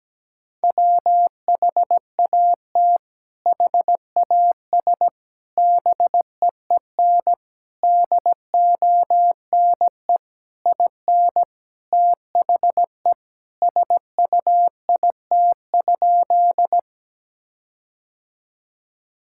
Morse code, heard as a keyed tone, about 17 wpm, 700 Hz. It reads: WHAT HAS BEEN DONE IN THE SUIT?
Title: Sentences from Top 1600 Words 17wpm